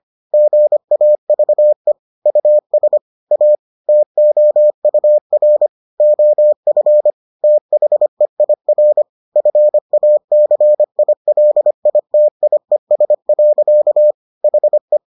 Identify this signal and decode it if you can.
Morse code, heard as a keyed tone, about 25 wpm, 600 Hz. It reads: GAVE US A TOUR OF THEIR FACILITIES. HE